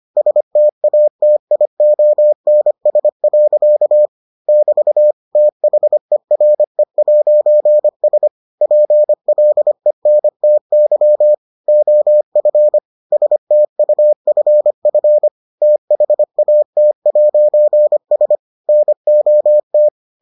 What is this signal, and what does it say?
Morse code, heard as a keyed tone, about 25 wpm, 600 Hz. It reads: STATIONS. <BT> THERE'S PLENTY OF STUFF THAT'S NOT